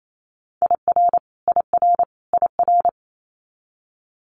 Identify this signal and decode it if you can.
Morse code, heard as a keyed tone, about 28 wpm, 700 Hz. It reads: IR IR IR
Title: Multi-letter Phonemes Repeated 3x 28wpm